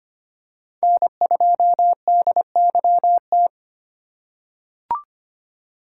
Morse code, heard as a keyed tone, about 25 wpm, 700 Hz. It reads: N2DYT
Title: K1USN SST QSO Practice - 25wpm